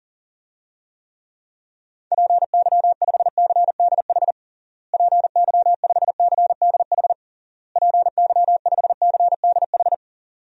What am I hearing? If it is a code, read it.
Morse code, heard as a keyed tone, about 37 wpm, 700 Hz. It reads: PY5CDH PY5CDH PY5CDH
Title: DXCC-100 Call Signs - 6 Characters - Repeated 3x - 40wpm